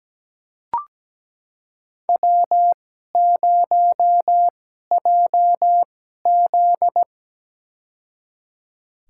Morse code, heard as a keyed tone, about 17 wpm, spaced as for 12 wpm, 700 Hz. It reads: W0JZ
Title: USA Call Signs - 4 Characters - 2x Character Spacing - 17wpm